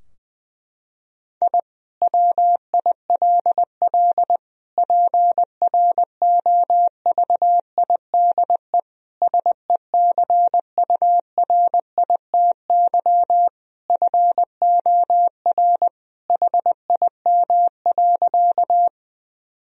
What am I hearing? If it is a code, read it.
Morse code, heard as a keyed tone, about 21 wpm, 700 Hz. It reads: I WILL PROVIDE SECURITY FOR HIM.
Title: Sentences from Top 1000 Words - Review Part 4 - 20wpm